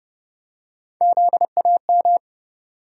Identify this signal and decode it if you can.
Morse code, heard as a keyed tone, about 32 wpm, 700 Hz. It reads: ZAM
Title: USA Call Sign Postfixes - 3 Letters 30wpm